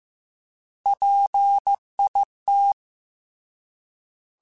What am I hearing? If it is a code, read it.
Morse code, heard as a keyed tone, about 15 wpm, 800 Hz. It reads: PIT
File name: Alphabetical / Alphabetical mp3s / Ps @13